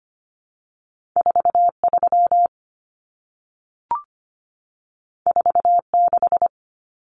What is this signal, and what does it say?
Morse code, heard as a keyed tone, about 25 wpm, 700 Hz. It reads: 43 46
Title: Common Cut Numbers - 2-Digit 25wpm